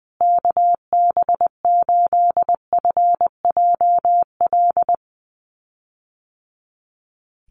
Morse code, heard as a keyed tone, about 20 wpm, 700 Hz. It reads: KB8FJL
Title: USA Call Signs Repeated 3x 20wpm